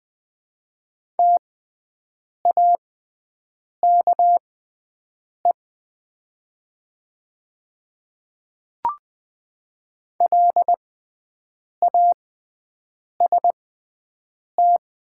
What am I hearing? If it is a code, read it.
Morse code, heard as a keyed tone, about 20 wpm, spaced as for 7 wpm, 700 Hz. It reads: TAKE LAST
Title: 6x Character Spacing - Top 500 Words - 20wpm